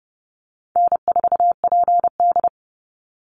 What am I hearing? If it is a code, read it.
Morse code, heard as a keyed tone, about 30 wpm, 700 Hz. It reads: N4PD
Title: Call Signs Repeated 2x - 30wpm